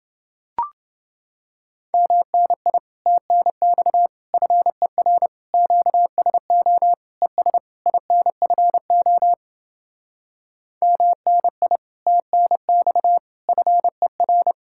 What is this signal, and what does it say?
Morse code, heard as a keyed tone, about 29 wpm, 700 Hz. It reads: MNI TNX FER QSO ES INFO MNI TNX FER